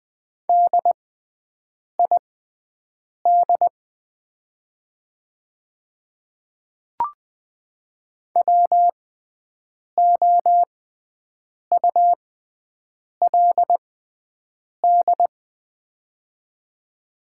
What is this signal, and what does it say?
Morse code, heard as a keyed tone, about 20 wpm, spaced as for 7 wpm, 700 Hz. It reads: DID WOULD
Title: 6x Character Spacing - Top 100 Words - 20wpm